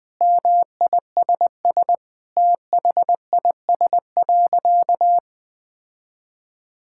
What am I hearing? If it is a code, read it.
Morse code, heard as a keyed tone, about 20 wpm, 700 Hz. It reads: MISS THIS.
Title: Sentences with 4-Word Phrases - 20wpm